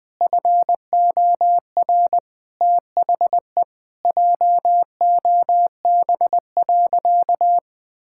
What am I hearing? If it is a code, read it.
Morse code, heard as a keyed tone, about 20 wpm, 700 Hz. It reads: FOR THE JOB.
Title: Sentences from Top 1600 Words 20wpm